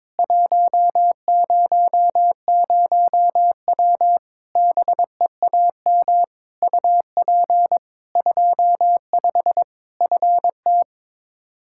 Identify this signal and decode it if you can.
Morse code, heard as a keyed tone, about 22 wpm, 700 Hz. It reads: 100W BEAM UP 25 FT
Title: K1USN SST QSO Practice 22wpm